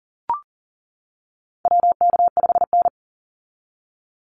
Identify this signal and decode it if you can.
Morse code, heard as a keyed tone, about 40 wpm, 700 Hz. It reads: WK5N
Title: Call Signs - 40wpm